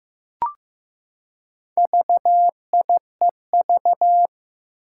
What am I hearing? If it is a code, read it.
Morse code, heard as a keyed tone, about 15 wpm, 700 Hz. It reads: VIEV
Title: Four Challenging Letter-Number Combinations 15wpm